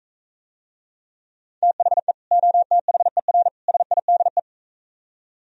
Morse code, heard as a keyed tone, about 44 wpm, 700 Hz. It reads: THE OTHER SIDE